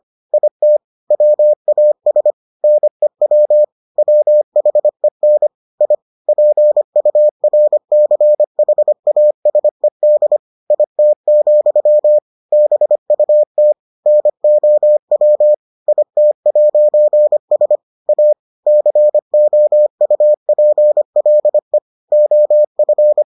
Morse code, heard as a keyed tone, about 25 wpm, 600 Hz. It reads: IT WAS NEW WHEN I PURCHASED IT, BUT NOW IT'S A COUPLE OF